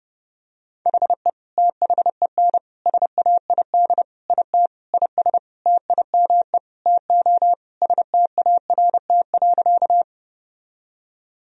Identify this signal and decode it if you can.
Morse code, heard as a keyed tone, about 30 wpm, 700 Hz. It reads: HE THEN SAID IT IS TIME TO START.
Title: Sentences from Top 1000 Words - Review Part 1 - 30wpm